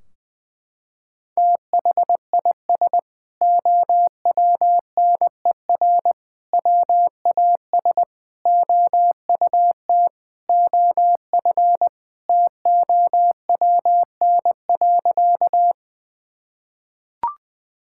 Morse code, heard as a keyed tone, about 20 wpm, 700 Hz. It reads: THIS OWNER WAS OUT OF TOWN.
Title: Sentences from Top 1600 Words 20wpm